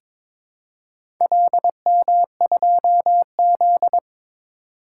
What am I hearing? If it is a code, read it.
Morse code, heard as a keyed tone, about 22 wpm, 700 Hz. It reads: LM2Z
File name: practice / 22wpm / Four Letter-Number Combinations - 22wpm